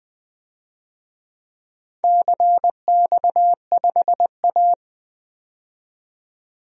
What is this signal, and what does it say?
Morse code, heard as a keyed tone, about 21 wpm, 700 Hz. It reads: CX5A
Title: DXCC-100 Call Signs - 4 Characters - 20wpm